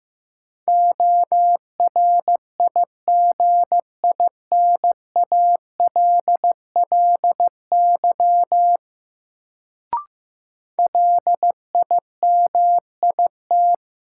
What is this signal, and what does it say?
Morse code, heard as a keyed tone, about 16 wpm, 700 Hz. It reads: ORIGINALLY LIMIT
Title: Sentences from Top 1500 Words - New Word Drill Alt - 15wpm